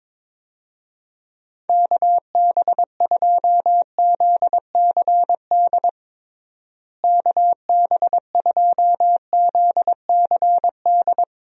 Morse code, heard as a keyed tone, about 22 wpm, 700 Hz. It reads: KB2ZCD KB2ZCD
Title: USA Call Signs - 6 Characters - Repeated 2x - 22wpm